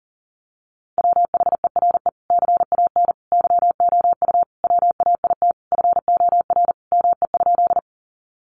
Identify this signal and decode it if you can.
Morse code, heard as a keyed tone, about 37 wpm, 700 Hz. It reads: WHERE CAN YOU WAIT FOR ME?
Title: Sentences from Top 500 Words 40wpm